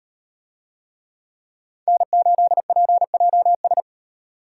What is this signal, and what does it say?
Morse code, heard as a keyed tone, about 38 wpm, 700 Hz. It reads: N8PJS